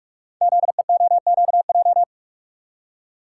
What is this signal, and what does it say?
Morse code, heard as a keyed tone, about 46 wpm, 700 Hz. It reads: ZEOQJ
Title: Code Groups - 5 Letters - 45wpm